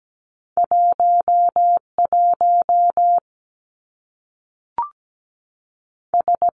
Morse code, heard as a keyed tone, about 17 wpm, 700 Hz. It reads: 1 1 S
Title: Two Challenging Letter-Number Combinations 17wpm